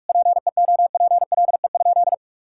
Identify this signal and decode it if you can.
Morse code, heard as a keyed tone, about 41 wpm, 700 Hz. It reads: PEOPLE?
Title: Sentences from Top 1000 Words - Review Part 6 - 45wpm